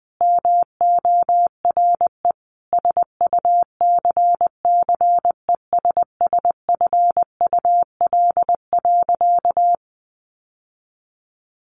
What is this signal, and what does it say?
Morse code, heard as a keyed tone, about 20 wpm, 700 Hz. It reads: MORE SUCCESSFUL.
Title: Sentences from Top 1500 Words 20wpm